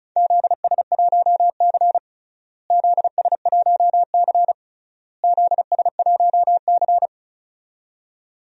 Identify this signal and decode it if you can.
Morse code, heard as a keyed tone, about 36 wpm, 700 Hz. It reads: ZS1C ZS1C ZS1C
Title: DXCC-100 Call Signs - 4 Characters - Repeated 3x - 35wpm